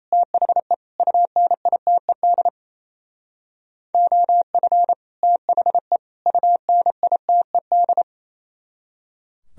Morse code, heard as a keyed tone, about 36 wpm, 700 Hz. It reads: THE UNITED OF THE UNITED